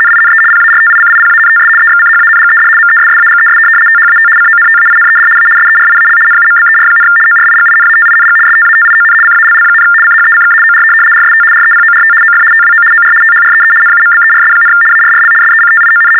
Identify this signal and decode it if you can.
RTTY, about 75 baud. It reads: NINE PARTICIPANTS EXCHANGED MESSAGES THAT CONTAINED A SERIAL NUMBER, ORIGINATING STATION CALL, CHECK OR RST REPORT OF TWO OR THREE NUMBERS, ARRL SECTION
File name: teletype-RTTY-75